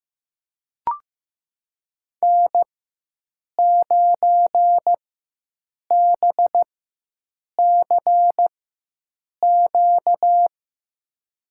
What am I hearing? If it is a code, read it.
Morse code, heard as a keyed tone, about 15 wpm, spaced as for 7 wpm, 700 Hz. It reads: N9BCQ
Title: USA Call Signs - 5 Characters - 4x Character Spacing - 15wpm